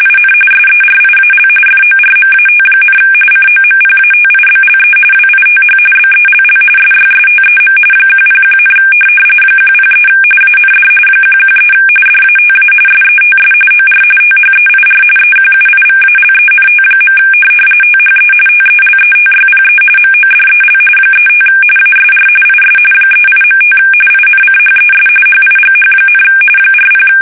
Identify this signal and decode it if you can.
RTTY, about 45.45 baud. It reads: IN SERIAL FORMAT, COMPATIBLE WITH BELL SYSTEM 101, 103 AND 113 DATA SETS OR THEIR EQUIVALENT. THE TELETYPE MODEL 35 ASR IS 38.5 INCHES HIGH, 40
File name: fsk850 / teletype-corp